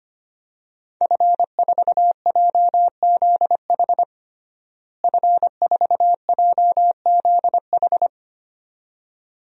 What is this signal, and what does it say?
Morse code, heard as a keyed tone, about 25 wpm, 700 Hz. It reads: F4JZH F4JZH
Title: DXCC-100 Call Signs - 5 Characters - Repeated 2x - 25wpm